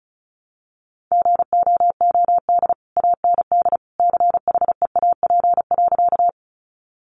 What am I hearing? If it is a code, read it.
Morse code, heard as a keyed tone, about 34 wpm, 700 Hz. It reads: GOOD AND CHEAP.
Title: Sentences from Top 1200 Words 35wpm